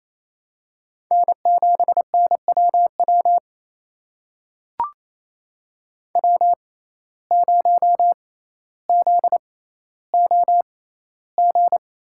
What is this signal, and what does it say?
Morse code, heard as a keyed tone, about 28 wpm, spaced as for 14 wpm, 700 Hz. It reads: N7NWW W0ZOG
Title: USA Call Signs - 5 Characters - 6x Character Spacing - 28wpm